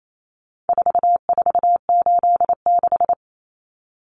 Morse code, heard as a keyed tone, about 28 wpm, 700 Hz. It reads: 4486
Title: Numbers - 4-Digit 28wpm